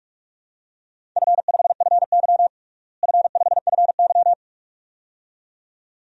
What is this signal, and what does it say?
Morse code, heard as a keyed tone, about 41 wpm, 700 Hz. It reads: F5FY F5FY
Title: DXCC-100 Call Signs - 4 Characters - Repeated 2x - 45wpm